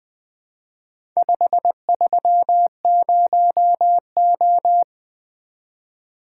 Morse code, heard as a keyed tone, about 20 wpm, 700 Hz. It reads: 530O
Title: Four Letter-Number Combinations - 20wpm